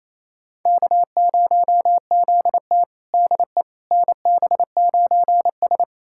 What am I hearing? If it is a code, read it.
Morse code, heard as a keyed tone, about 28 wpm, 700 Hz. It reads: K0ZT DE NB9S
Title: USA Callsign DE Callsign - 4 Characters - 28wpm